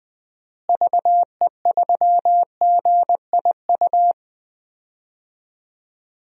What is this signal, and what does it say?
Morse code, heard as a keyed tone, about 20 wpm, 700 Hz. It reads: VE3GIU
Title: DXCC-100 Call Signs - 6 Characters - Repeated 3x - 20wpm